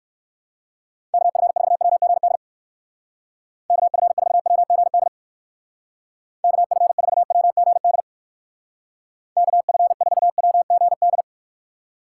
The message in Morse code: KF4WGD KF4WGD KF4WGD KF4WGD

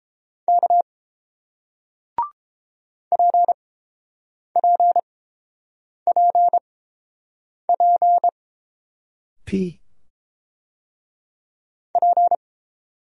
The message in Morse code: K PPPPP

